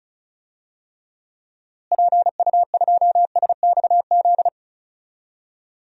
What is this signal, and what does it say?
Morse code, heard as a keyed tone, about 34 wpm, 700 Hz. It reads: PU2SXZ
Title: DXCC-100 Call Signs - 6 Characters - Repeated 2x - 35wpm